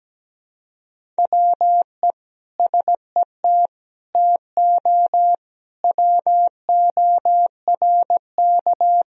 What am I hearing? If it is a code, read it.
Morse code, heard as a keyed tone, about 17 wpm, 700 Hz. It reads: WE SET TO WORK
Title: Sentences from Top 1000 Words - Review Part 1 - 17wpm